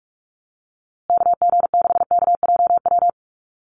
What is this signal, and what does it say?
Morse code, heard as a keyed tone, about 44 wpm, 700 Hz. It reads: KG6KJW
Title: USA Call Signs Repeated 2x 45wpm